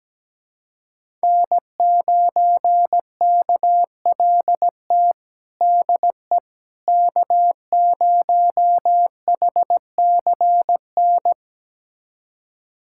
Morse code, heard as a keyed tone, about 17 wpm, 700 Hz. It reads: N9KLT DE K0HCN
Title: USA Callsign DE Callsign - 5 Characters - 17wpm